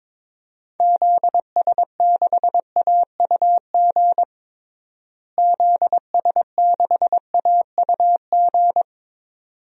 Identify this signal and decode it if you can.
Morse code, heard as a keyed tone, about 22 wpm, 700 Hz. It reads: ZS6AUG ZS6AUG
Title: DXCC-100 Call Signs - 6 Characters - Repeated 2x - 22wpm